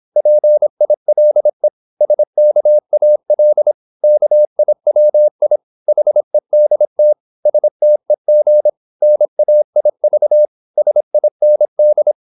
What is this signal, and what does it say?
Morse code, heard as a keyed tone, about 26 wpm, 600 Hz. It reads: PILE SKAL KIWI HEDT STEG NAIV SIND
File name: Korte ord DK 26wpm